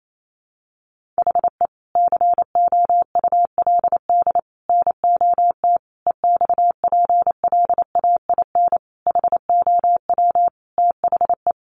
Morse code, heard as a keyed tone, about 28 wpm, 700 Hz. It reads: HE COULD NOT EXPLAIN HOW THE
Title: Sentences from Top 1500 Words 28wpm